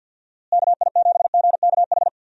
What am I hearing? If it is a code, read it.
Morse code, heard as a keyed tone, about 48 wpm, 700 Hz. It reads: KI7GKH
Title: USA Call Signs - 6 Characters - Repeated 2x - 50wpm